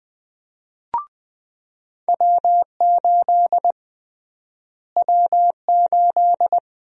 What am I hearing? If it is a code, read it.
Morse code, heard as a keyed tone, about 20 wpm, 700 Hz. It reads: W8 W8
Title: USA Call Sign Prefixes - Repeated 2x - 20wpm